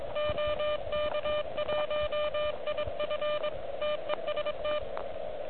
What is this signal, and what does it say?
Morse code, heard as a keyed tone, about 22 wpm, 600 Hz, with radio noise. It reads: OK1IF TEST
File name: OK QRP 2022 / Zvuk / REC37